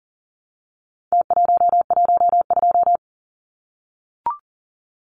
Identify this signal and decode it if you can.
Morse code, heard as a keyed tone, about 38 wpm, 700 Hz. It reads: T112
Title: Numbers - Contest Style - 4-Digit 40wpm